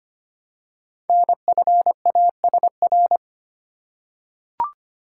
Morse code, heard as a keyed tone, about 25 wpm, 700 Hz. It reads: NFASR
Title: Code Groups - 5 Letters - 25wpm